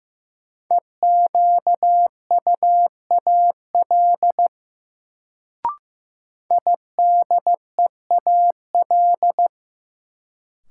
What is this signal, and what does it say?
Morse code, heard as a keyed tone, about 15 wpm, 700 Hz. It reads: EQUAL IDEAL
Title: Sentences from Top 1400 Words - New Word Drill - 15wpm